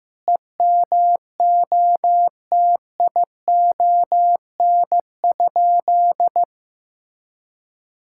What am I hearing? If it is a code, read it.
Morse code, heard as a keyed tone, about 15 wpm, 700 Hz. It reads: EMOTION?
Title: Sentences from Top 1600 Words 15wpm